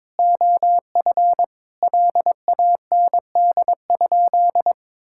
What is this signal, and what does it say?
Morse code, heard as a keyed tone, about 22 wpm, 700 Hz. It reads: OF LAND?
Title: Sentences from Top 1600 Words 22wpm